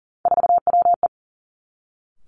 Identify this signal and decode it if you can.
Morse code, heard as a keyed tone, about 40 wpm, 700 Hz. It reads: 4WE
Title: Three Challenging Letter-Number Combinations 40wpm